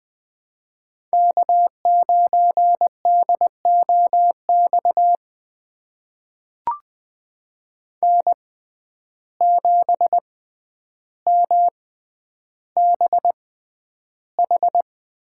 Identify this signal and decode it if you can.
Morse code, heard as a keyed tone, about 20 wpm, spaced as for 10 wpm, 700 Hz. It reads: K9DOX N7MBH
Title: USA Call Signs - 5 Characters - 6x Character Spacing - 20wpm